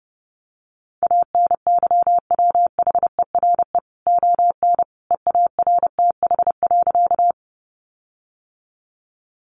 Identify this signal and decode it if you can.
Morse code, heard as a keyed tone, about 30 wpm, 700 Hz. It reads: ANYWHERE ON EARTH.
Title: Sentences from Top 1400 Words 30wpm